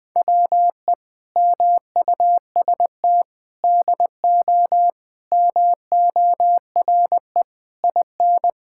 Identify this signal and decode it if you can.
Morse code, heard as a keyed tone, about 21 wpm, 700 Hz. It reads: WE MUST DO MORE IN
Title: Sentences from Top 1000 Words - Review Part 5 - 20wpm